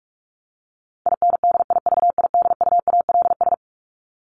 Morse code, heard as a keyed tone, about 44 wpm, 700 Hz. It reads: INDIVIDUALS
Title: ES:S Suffix - Top 100 Words with Base Words 45wpm